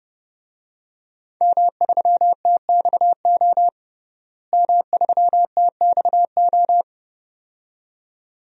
Morse code, heard as a keyed tone, about 30 wpm, 700 Hz. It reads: M3TXO M3TXO
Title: DXCC-100 Call Signs - 5 Characters - Repeated 2x - 30wpm